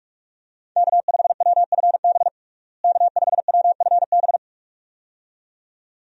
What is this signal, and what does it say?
Morse code, heard as a keyed tone, about 46 wpm, 700 Hz. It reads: K5WFB K5WFB
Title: USA Call Signs - 5 Characters - Repeated 2x - 45wpm